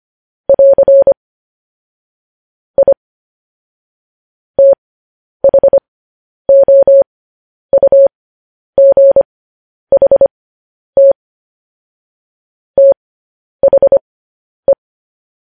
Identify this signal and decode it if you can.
Morse code, heard as a keyed tone, about 25 wpm, spaced as for 10 wpm, 550 Hz. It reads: <AR> I THOUGHT THE
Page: news-headlines-in-morse-code-at-10-wpm